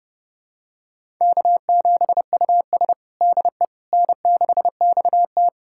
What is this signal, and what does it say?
Morse code, heard as a keyed tone, about 30 wpm, 700 Hz. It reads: K7US DE N6XT
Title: USA Callsign DE Callsign - 4 Characters - 30wpm